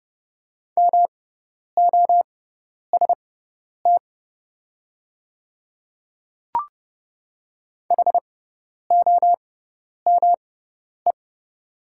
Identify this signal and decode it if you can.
Morse code, heard as a keyed tone, about 30 wpm, spaced as for 10 wpm, 690 Hz. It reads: MOST HOME
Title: 6x Character Spacing - Top 200 Words - 30wpm